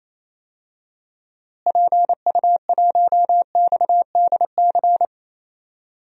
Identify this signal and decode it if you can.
Morse code, heard as a keyed tone, about 27 wpm, 700 Hz. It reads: PU1XDC
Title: DXCC-100 Call Signs - 6 Characters - Repeated 2x - 28wpm